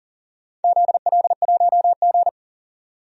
Morse code, heard as a keyed tone, about 40 wpm, 700 Hz. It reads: ZL1G
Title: DXCC-100 Call Signs - 4 Characters - 40wpm